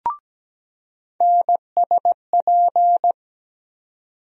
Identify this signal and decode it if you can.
Morse code, heard as a keyed tone, about 17 wpm, 700 Hz. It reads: NSP
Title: USA Call Sign Postfixes - 3 Letters 17wpm